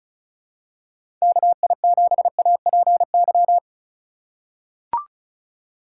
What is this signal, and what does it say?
Morse code, heard as a keyed tone, about 36 wpm, 700 Hz. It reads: KI7APY
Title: USA Call Signs - 6 Characters - 2x Character Spacing - 35wpm